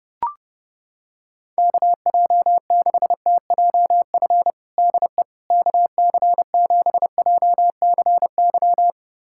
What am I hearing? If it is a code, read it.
Morse code, heard as a keyed tone, about 29 wpm, 700 Hz. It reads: KJ6TJF DE KC7JCY
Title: USA Callsign DE Callsign - 6 Characters - 30wpm